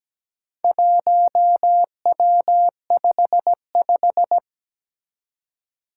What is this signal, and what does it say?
Morse code, heard as a keyed tone, about 17 wpm, 700 Hz. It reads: 1W55
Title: Four Challenging Letter-Number Combinations 17wpm